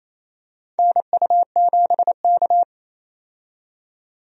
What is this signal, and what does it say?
Morse code, heard as a keyed tone, about 27 wpm, 700 Hz. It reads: NU7K
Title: USA Call Signs - 4 Characters - 28wpm